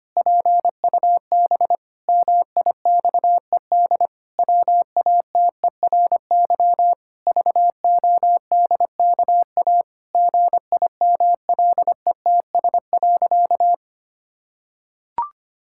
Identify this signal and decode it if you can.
Morse code, heard as a keyed tone, about 25 wpm, 700 Hz. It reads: PUB MIXED WATERY VODKA GIMLETS.
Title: Pangram Challenge v2 - 25wpm